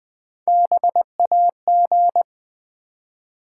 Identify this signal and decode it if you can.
Morse code, heard as a keyed tone, about 20 wpm, 700 Hz. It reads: BAG
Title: Easy Words with Hints - 20wpm